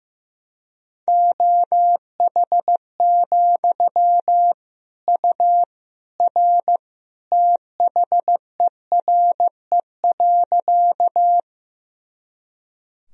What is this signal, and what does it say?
Morse code, heard as a keyed tone, about 15 wpm, 700 Hz. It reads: OH, U R THERE.
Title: Sentences from Top 500 Words and Abbreviations 15wpm